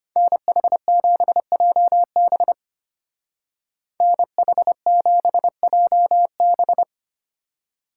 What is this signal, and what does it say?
Morse code, heard as a keyed tone, about 30 wpm, 700 Hz. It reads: NH7JB NH7JB
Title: Speed Racing - USA 2x2 Call Signs 25wpm